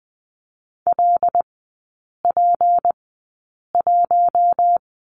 Morse code, heard as a keyed tone, about 20 wpm, spaced as for 8 wpm, 700 Hz. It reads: LP1
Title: 152 Lesson 25e - Letter p - Three Characters 20wpm